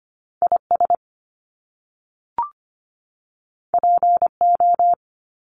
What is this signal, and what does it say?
Morse code, heard as a keyed tone, about 25 wpm, 700 Hz. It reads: IS PO
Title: Top Two Letter Combinations in Words 25wpm